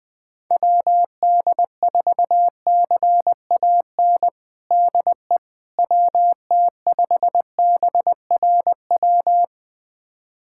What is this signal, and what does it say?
Morse code, heard as a keyed tone, about 20 wpm, 700 Hz. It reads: WD4CAN DE WT5BRW
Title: USA Callsign DE Callsign 20wpm